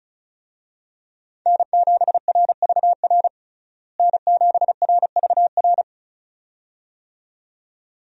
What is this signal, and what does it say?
Morse code, heard as a keyed tone, about 36 wpm, 700 Hz. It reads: N7RVR N7RVR
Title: USA Call Signs - 5 Characters - Repeated 2x - 35wpm